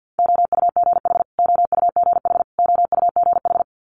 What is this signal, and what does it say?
Morse code, heard as a keyed tone, about 50 wpm, 690 Hz. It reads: OUGH OUGH OUGH
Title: Multi-letter Phonemes Repeated 3x 50wpm